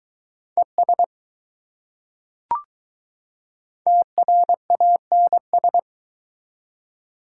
Morse code, heard as a keyed tone, about 22 wpm, 700 Hz. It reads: ES TRANS